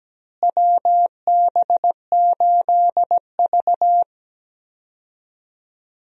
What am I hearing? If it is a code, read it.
Morse code, heard as a keyed tone, about 17 wpm, 700 Hz. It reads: WB8V
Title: USA Call Signs - 4 Characters - Repeated 3x - 17wpm